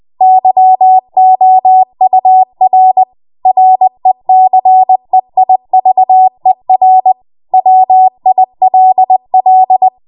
Morse code, heard as a keyed tone, about 20 wpm, 750 Hz. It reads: YOUR RECEIVER WILL
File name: Morse / 20WPM